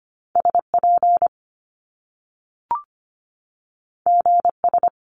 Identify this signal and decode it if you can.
Morse code, heard as a keyed tone, about 25 wpm, 700 Hz. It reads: SP GS
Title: Top Two Letter Combinations in Words 25wpm